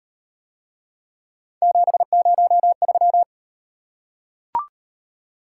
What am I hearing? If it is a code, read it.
Morse code, heard as a keyed tone, about 35 wpm, 700 Hz. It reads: 703